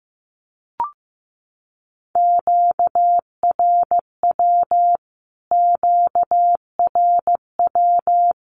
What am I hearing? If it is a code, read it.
Morse code, heard as a keyed tone, about 15 wpm, 700 Hz. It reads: QRW QRW
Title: Q-Codes Repeated 3x 15wpm